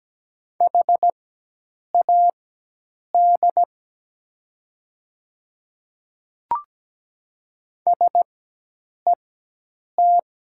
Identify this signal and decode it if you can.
Morse code, heard as a keyed tone, about 17 wpm, spaced as for 8 wpm, 700 Hz. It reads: HAD SET
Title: 4x Character Spacing - Top 200 Words - 17wpm